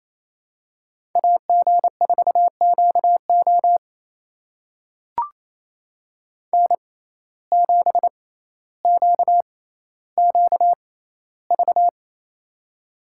Morse code, heard as a keyed tone, about 28 wpm, spaced as for 14 wpm, 700 Hz. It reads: AG4QO N7QQV
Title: USA Call Signs - 5 Characters - 6x Character Spacing - 28wpm